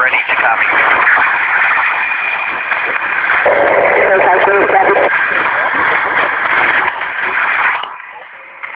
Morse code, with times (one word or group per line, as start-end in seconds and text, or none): none